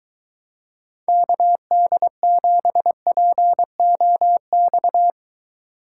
Morse code, heard as a keyed tone, about 23 wpm, 700 Hz. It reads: KD7POX